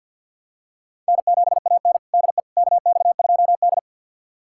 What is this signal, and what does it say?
Morse code, heard as a keyed tone, about 47 wpm, 700 Hz. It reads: N7AN DE KX2D
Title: USA Callsign DE Callsign - 4 Characters - 50wpm